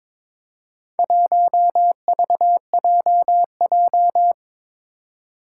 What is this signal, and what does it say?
Morse code, heard as a keyed tone, about 21 wpm, 700 Hz. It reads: 1VJJ
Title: Four Challenging Letter-Number Combinations 22wpm